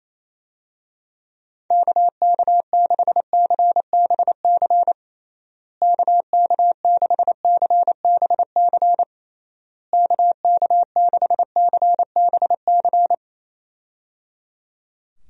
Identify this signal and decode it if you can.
Morse code, heard as a keyed tone, about 28 wpm, 700 Hz. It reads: KK6CBC KK6CBC KK6CBC
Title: USA Call Signs - 6 Characters - Repeated 3x - 28wpm